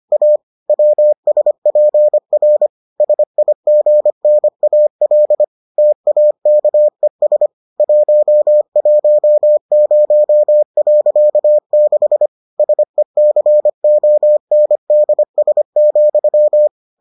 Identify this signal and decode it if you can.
Morse code, heard as a keyed tone, about 25 wpm, 600 Hz. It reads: A WSPR SIGNAL TAKES 110.6 SECONDS,